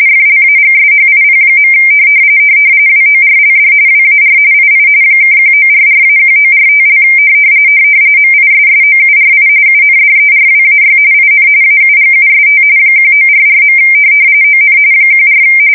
RTTY, about 45.45 baud. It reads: FLAGSHIP. WORLD WAR II (1944-1945) MISSOURI UNDERWAY IN AUGUST 1944. ON 14
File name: fsk170 / USS-Missouri